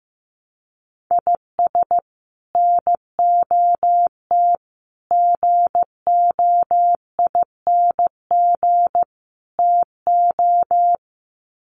Morse code, heard as a keyed tone, about 15 wpm, 700 Hz. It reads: IS NOT GOING TO
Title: Sets of 4 Words 15wpm